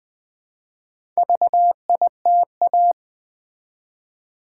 Morse code, heard as a keyed tone, about 20 wpm, 700 Hz. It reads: VITA
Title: Build A Word - 1 Letter At A Time - 5-Letter Words - 20wpm